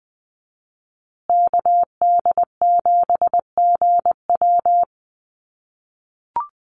Morse code, heard as a keyed tone, about 20 wpm, 700 Hz. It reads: KD7GW
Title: USA Call Signs Repeated 2x 20wpm